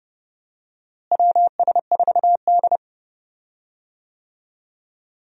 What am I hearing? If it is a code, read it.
Morse code, heard as a keyed tone, about 30 wpm, 700 Hz. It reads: WS4D
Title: USA Call Signs - 4 Characters - Repeated 2x - 30wpm